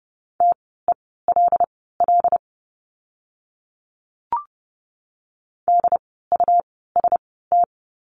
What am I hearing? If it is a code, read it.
Morse code, heard as a keyed tone, about 30 wpm, spaced as for 17 wpm, 700 Hz. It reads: TELL DUST
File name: Lesson 17f - Letter u - Words 30wpm